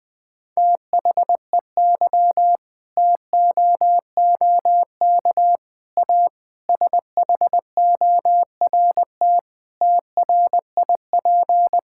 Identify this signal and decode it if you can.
Morse code, heard as a keyed tone, about 20 wpm, 700 Hz. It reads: THEY TOOK A SHORT TRIP
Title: Difficult Sentences with Hints - 20wpm